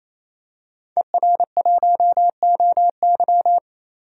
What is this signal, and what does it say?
Morse code, heard as a keyed tone, about 28 wpm, 700 Hz. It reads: ER1OY
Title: DXCC-100 Call Signs - 5 Characters - 28wpm